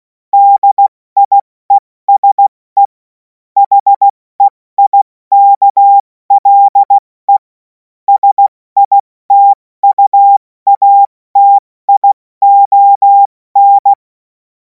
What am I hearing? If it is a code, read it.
Morse code, heard as a keyed tone, about 16 wpm, 800 Hz. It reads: DIESE HEIKLE SITUATION
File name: Morsekurs / Witze / Witz 07. April 2026 70 F80